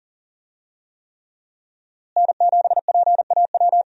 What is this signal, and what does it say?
Morse code, heard as a keyed tone, about 37 wpm, 700 Hz. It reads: N7PAW
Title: USA Call Signs - 5 Characters - Repeated 3x - 40wpm